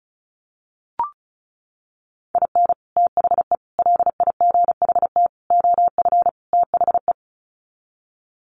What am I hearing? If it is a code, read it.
Morse code, heard as a keyed tone, about 34 wpm, 700 Hz. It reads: IN THE LIGHT OF THE
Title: Sets of 5 Words Encore2 35wpm